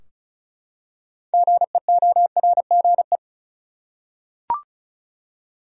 Morse code, heard as a keyed tone, about 36 wpm, 700 Hz. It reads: GEORGE
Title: USA Common Names - 35wpm